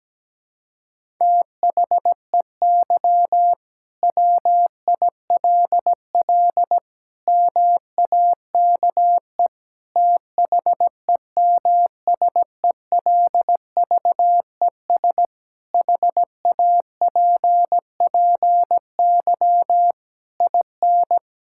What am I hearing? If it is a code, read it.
Morse code, heard as a keyed tone, about 17 wpm, 700 Hz. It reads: THEY WILL MAKE THEMSELVES HAPPY IN
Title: Sentences from Top 1600 Words 17wpm